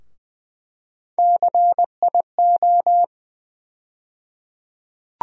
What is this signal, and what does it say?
Morse code, heard as a keyed tone, about 21 wpm, 700 Hz. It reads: CIO
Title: USA Call Sign Postfixes - 3 Letters 20wpm